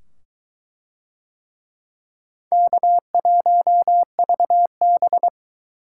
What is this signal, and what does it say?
Morse code, heard as a keyed tone, about 21 wpm, 700 Hz. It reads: K1VB